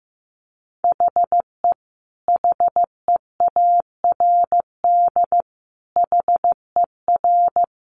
Morse code, heard as a keyed tone, about 15 wpm, 700 Hz. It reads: HE HEARD HER
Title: Sentences from Top 1100 Words 15wpm